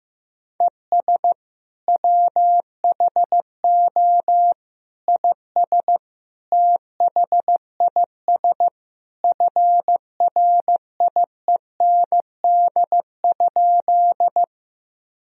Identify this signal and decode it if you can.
Morse code, heard as a keyed tone, about 15 wpm, 700 Hz. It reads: ES WHO IS THIS FRIEND?
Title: Sentences from Top 300 Words and Abbreviations 15wpm